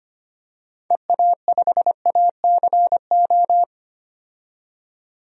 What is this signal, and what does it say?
Morse code, heard as a keyed tone, about 25 wpm, 700 Hz. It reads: EA5ACO
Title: DXCC-100 Call Signs - 6 Characters - 25wpm